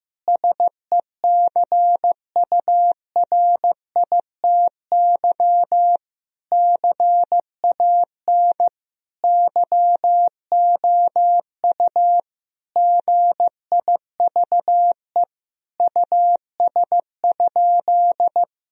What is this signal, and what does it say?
Morse code, heard as a keyed tone, about 16 wpm, 700 Hz. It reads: SECURITY CAN YOU GIVE US?
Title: Sentences from Top 1000 Words - Review Part 4 - 15wpm